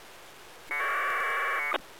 Packet radio (AX.25, as in APRS).